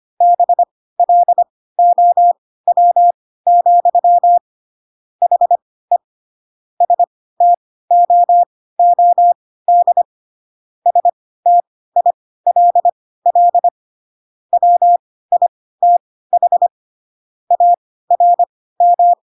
Morse code, heard as a keyed tone, about 25 wpm, spaced as for 16 wpm, 700 Hz. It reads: BLOW, HE STOOD STILL WITH ARM
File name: Pinocchio / Pinocchio ch1 10 25 ik1hge